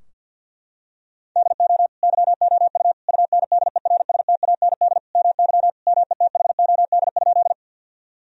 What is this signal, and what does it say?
Morse code, heard as a keyed tone, about 50 wpm, 700 Hz. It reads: DO YOU UNDERSTAND MY METHOD?
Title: Sentences from Top 1000 Words - Review Part 3 - 50wpm